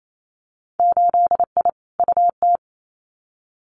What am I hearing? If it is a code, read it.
Morse code, heard as a keyed tone, about 28 wpm, 690 Hz. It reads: 8I UT
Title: Winter Field Day Exchanges Encore 28wpm